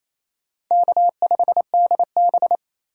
Morse code, heard as a keyed tone, about 27 wpm, 700 Hz. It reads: K5DB
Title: USA Call Signs - 4 Characters - 4x Character Spacing - 28wpm